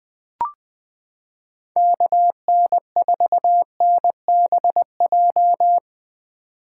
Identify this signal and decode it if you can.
Morse code, heard as a keyed tone, about 20 wpm, 700 Hz. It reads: KN4NBJ
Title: 2023 POTA Activator Call Signs - 20wpm